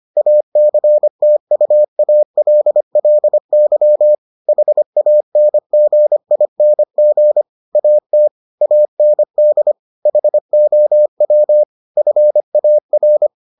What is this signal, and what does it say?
Morse code, heard as a keyed tone, about 25 wpm, 600 Hz. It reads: ACTUALLY HANGING AT AND HOW FAR